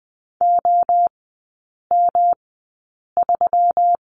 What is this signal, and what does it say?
Morse code, heard as a keyed tone, about 20 wpm, spaced as for 8 wpm, 700 Hz. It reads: OM3
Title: 152 Lesson 25e - Letter p - Three Characters 20wpm